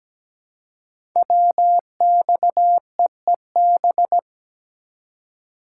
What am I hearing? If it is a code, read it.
Morse code, heard as a keyed tone, about 17 wpm, 700 Hz. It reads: WXEEB
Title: Code Groups - 5 Letters - 17wpm